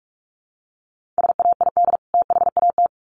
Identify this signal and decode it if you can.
Morse code, heard as a keyed tone, about 44 wpm, 700 Hz. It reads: SAID THAT
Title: Sets of 2 Words 45wpm